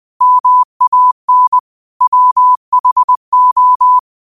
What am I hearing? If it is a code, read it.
Morse code, heard as a keyed tone, about 20 wpm, 1000 Hz. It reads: MAN WHO
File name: qotdfiles / 20wpm